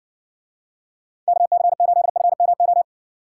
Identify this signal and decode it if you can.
Morse code, heard as a keyed tone, about 60 wpm, 700 Hz. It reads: KC8FMO